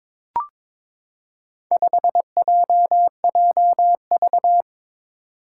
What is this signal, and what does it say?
Morse code, heard as a keyed tone, about 22 wpm, 700 Hz. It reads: 5JJV
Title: Four Challenging Letter-Number Combinations 22wpm